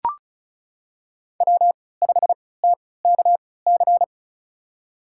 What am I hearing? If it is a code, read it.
Morse code, heard as a keyed tone, about 36 wpm, spaced as for 20 wpm, 700 Hz. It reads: W5TKC
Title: USA Call Signs - 5 Characters - 3x Character Spacing - 35wpm